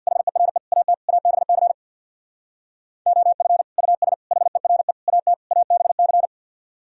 Morse code, heard as a keyed tone, about 64 wpm, 700 Hz. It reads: HERE AT ABC OF US HERE AT ABC